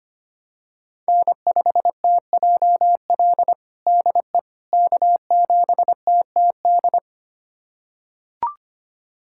Morse code, heard as a keyed tone, about 25 wpm, 700 Hz. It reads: N5TJL DE K7TTD
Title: USA Callsign DE Callsign - 5 Characters - 25wpm